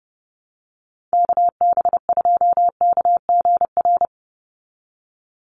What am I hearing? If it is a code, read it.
Morse code, heard as a keyed tone, about 30 wpm, 700 Hz. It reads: KB2KGR
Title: USA Call Signs 30wpm